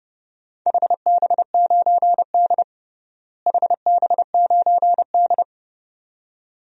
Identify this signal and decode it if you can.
Morse code, heard as a keyed tone, about 30 wpm, 700 Hz. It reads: HB9D HB9D
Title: DXCC-100 Call Signs - 4 Characters - Repeated 2x - 30wpm